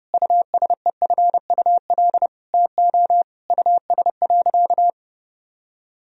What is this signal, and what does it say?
Morse code, heard as a keyed tone, about 32 wpm, 700 Hz. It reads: USEFUL TO US.
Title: Sentences from Top 1000 Words - Review Part 4 - 30wpm